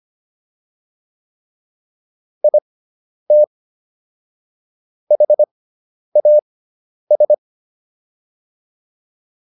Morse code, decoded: IT HAS

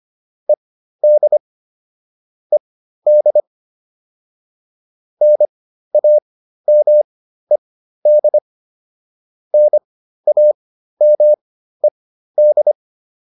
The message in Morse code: ED ED NAMED NAMED

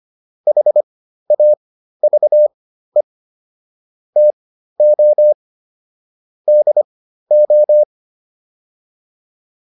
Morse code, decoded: HAVE TO DO

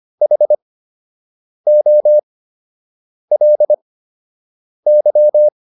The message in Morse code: HOLY

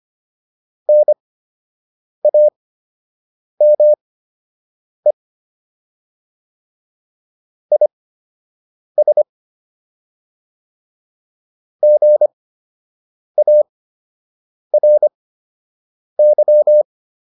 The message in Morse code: NAME IS GARY

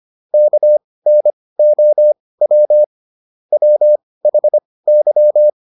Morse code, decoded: KNOW WHY

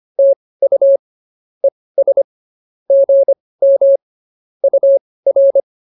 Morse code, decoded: TU ES GM UR

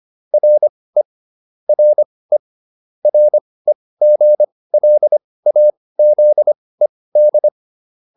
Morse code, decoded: RE RE REGLAZED

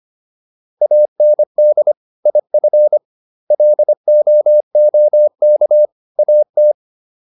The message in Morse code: AND IF LOOK AT

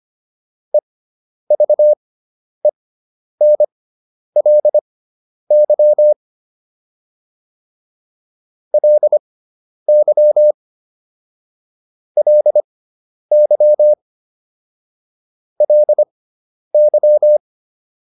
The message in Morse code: EVENLY LY LY LY

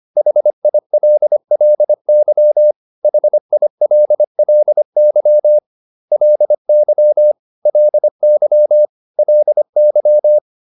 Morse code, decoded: HILLY HILLY LY LY LY

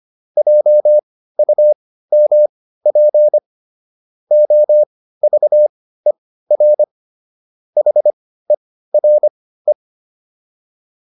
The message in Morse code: JUMP OVER HERE

